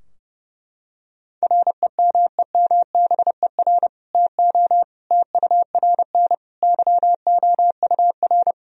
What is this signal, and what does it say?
Morse code, heard as a keyed tone, about 30 wpm, 700 Hz. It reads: REMEMBER TO TURN YOUR
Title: Sentences from Top 1000 Words - Review Part 3 - 30wpm